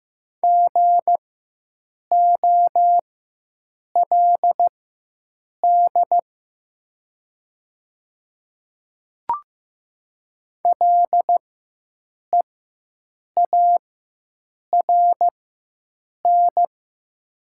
Morse code, heard as a keyed tone, about 15 wpm, spaced as for 7 wpm, 690 Hz. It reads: GOLD LEARN
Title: 4x Character Spacing - Top 500 Words - 15wpm